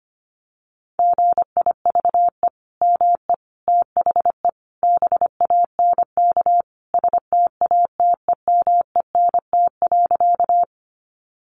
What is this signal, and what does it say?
Morse code, heard as a keyed tone, about 25 wpm, 700 Hz. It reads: GIVE ME THE BANK STATEMENT.
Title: Sentences from Top 1200 Words 25wpm